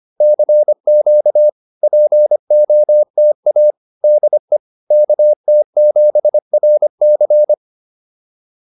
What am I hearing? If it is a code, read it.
Morse code, heard as a keyed tone, about 25 wpm, 600 Hz. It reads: CQ POTA DE KT7RC